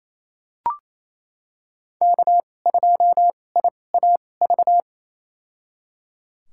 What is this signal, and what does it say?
Morse code, heard as a keyed tone, about 28 wpm, spaced as for 20 wpm, 700 Hz. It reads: K2IAV
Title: USA Call Signs - 5 Characters - 2x Character Spacing - 28wpm